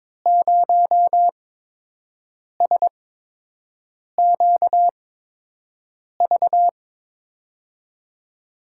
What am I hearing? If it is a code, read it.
Morse code, heard as a keyed tone, about 23 wpm, spaced as for 6 wpm, 700 Hz. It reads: 0SQV
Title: USA Call Signs - 5 Characters - 8x Character Spacing - 22wpm